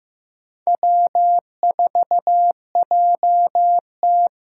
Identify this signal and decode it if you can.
Morse code, heard as a keyed tone, about 15 wpm, 700 Hz. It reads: W4JT
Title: USA Call Signs - 8x Character Spacing - 15wpm